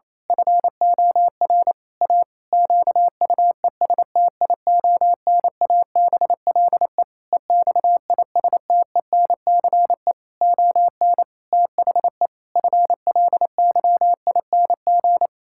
Morse code, heard as a keyed tone, about 27 wpm, 700 Hz. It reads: FOR A QUESTIONABLE EXISTENCE ON THE FLYING
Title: Pangram Challenge v2 - 28wpm